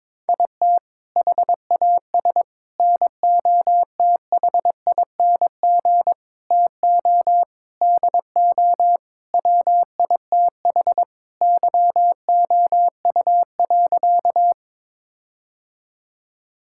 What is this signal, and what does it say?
Morse code, heard as a keyed tone, about 22 wpm, 700 Hz. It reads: IT HAS NOTHING TO DO WITH YOU.
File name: practice / 22wpm / Sentences with 4-Word Phrases - 22wpm